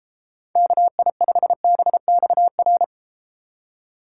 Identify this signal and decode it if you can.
Morse code, heard as a keyed tone, about 32 wpm, 700 Hz. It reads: KI5BXR